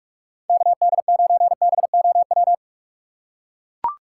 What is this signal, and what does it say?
Morse code, heard as a keyed tone, about 44 wpm, 700 Hz. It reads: KD9BOW
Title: USA Call Signs - 6 Characters - 3x Character Spacing - 45wpm